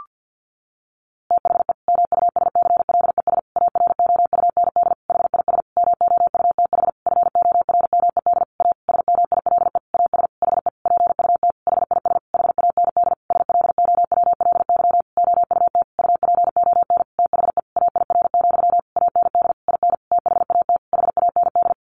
Morse code, heard as a keyed tone, about 50 wpm, 700 Hz. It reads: THE MUSCLES AROUND HIS MOUTH FORMED A SMILE AS HE PUT HIS HAND SLOWLY OUT UPON THE AIR, AND IN THAT HAND
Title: A Sound Of Thunder 50wpm